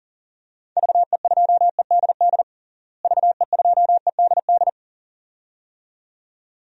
Morse code, heard as a keyed tone, about 40 wpm, 700 Hz. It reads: VE2EDD VE2EDD
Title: DXCC-100 Call Signs - 6 Characters - Repeated 2x - 40wpm